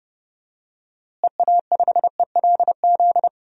Morse code, heard as a keyed tone, about 32 wpm, 700 Hz. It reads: EA5ELZ
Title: DXCC-100 Call Signs - 6 Characters - Repeated 3x - 30wpm